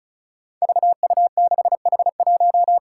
Morse code, heard as a keyed tone, about 36 wpm, 700 Hz. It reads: VU6H1
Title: Five Challenging Letter-Number Combinations 35wpm